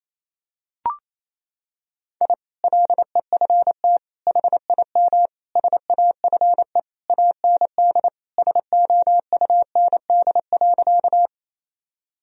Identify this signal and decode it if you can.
Morse code, heard as a keyed tone, about 28 wpm, 700 Hz. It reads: I LEFT HIM SAFE AND SOUND.
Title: Sentences from Top 1000 Words - Review Part 1 - 28wpm